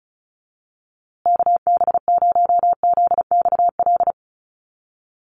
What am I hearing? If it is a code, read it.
Morse code, heard as a keyed tone, about 34 wpm, 700 Hz. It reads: KB0ZXL
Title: USA Call Signs Repeated 2x 35wpm